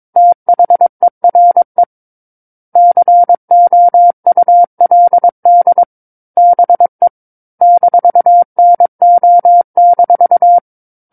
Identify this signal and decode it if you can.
Morse code, heard as a keyed tone, about 22 wpm, 700 Hz. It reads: THERE COULD BE -NO-
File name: 06-Chapter-06 22 WPM